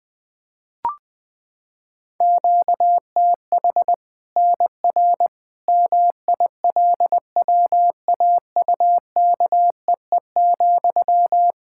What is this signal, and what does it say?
Morse code, heard as a keyed tone, about 21 wpm, 700 Hz. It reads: QTH NR MILWAUKEE,
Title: K1USN SST QSO Practice 20wpm